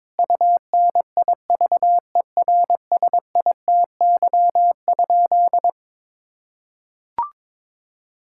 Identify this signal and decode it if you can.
Morse code, heard as a keyed tone, about 22 wpm, 700 Hz. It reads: UNIVERSITY?
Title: Sentences from Top 1000 Words - Review Part 6 - 22wpm